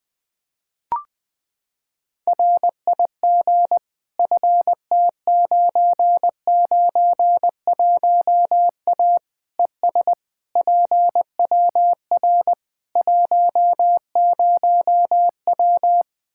Morse code, heard as a keyed tone, about 20 wpm, 700 Hz. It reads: RIG FT991A ES PWR 10W
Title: Practice QSO - Exchange 2 - Info - 20wpm